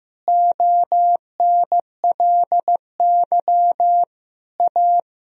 Morse code, heard as a keyed tone, about 15 wpm, 700 Hz. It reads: ONLY A
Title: Sentences from Top 1000 Words - Review Part 1 - 15wpm